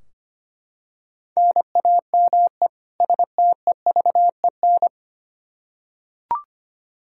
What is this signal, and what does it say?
Morse code, heard as a keyed tone, about 25 wpm, 700 Hz. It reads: NAME STEVEN
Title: Practice QSO - Exchange 1 - Beginning - 25wpm